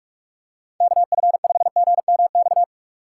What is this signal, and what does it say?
Morse code, heard as a keyed tone, about 41 wpm, 700 Hz. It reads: KF5GMX
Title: USA Call Signs - 6 Characters - 6x Character Spacing - 45wpm